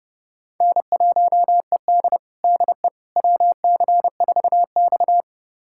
Morse code, heard as a keyed tone, about 30 wpm, 700 Hz. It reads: N1ED DE WC4X
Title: USA Callsign DE Callsign - 4 Characters - 30wpm